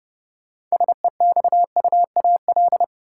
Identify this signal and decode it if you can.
Morse code, heard as a keyed tone, about 30 wpm, 700 Hz. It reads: SEXUAL
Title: Six Letter Words - 30wpm